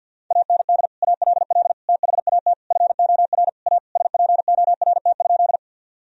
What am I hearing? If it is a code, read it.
Morse code, heard as a keyed tone, about 50 wpm, 700 Hz. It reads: AND ALL THAT FOR A SPORT?
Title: Sentences from Top 1000 Words - Review Part 4 - 50wpm